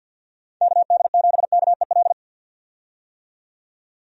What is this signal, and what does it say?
Morse code, heard as a keyed tone, about 51 wpm, 700 Hz. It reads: KD7KEL
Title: USA Call Signs - 6 Characters - 50wpm